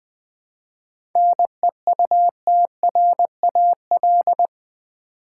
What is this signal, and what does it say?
Morse code, heard as a keyed tone, about 20 wpm, 700 Hz. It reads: NEUTRAL
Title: 7 Letter Words - First 3 Letters Given - 20wpm